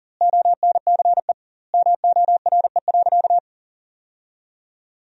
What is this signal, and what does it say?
Morse code, heard as a keyed tone, about 38 wpm, 700 Hz. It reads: ONCE MORE.
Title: 2x Word-Spacing - Sentences from Top 500 Words - 40wpm